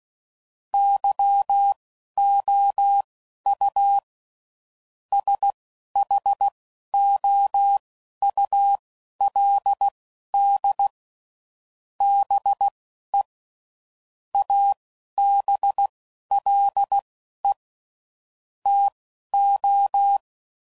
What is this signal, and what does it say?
Morse code, heard as a keyed tone, about 16 wpm, spaced as for 11 wpm, 800 Hz. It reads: YOU SHOULD BE ABLE TO
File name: CW Code Groups @10WPM / Technical Subjects / OR Gate